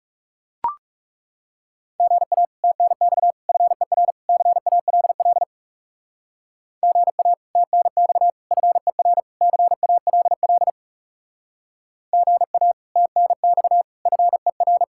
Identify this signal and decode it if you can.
Morse code, heard as a keyed tone, about 39 wpm, 700 Hz. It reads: GA TNX FER CALL GA TNX FER CALL GA TNX FER